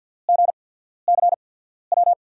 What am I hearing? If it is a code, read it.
Morse code, heard as a keyed tone, about 48 wpm, 700 Hz. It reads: G C W
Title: USA Call Signs - 6 Characters - 8x Character Spacing - 50wpm